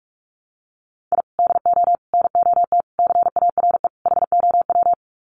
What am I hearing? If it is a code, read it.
Morse code, heard as a keyed tone, about 44 wpm, 700 Hz. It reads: I DO NOT CARE HOW
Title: Sets of 5 Words Encore2 45wpm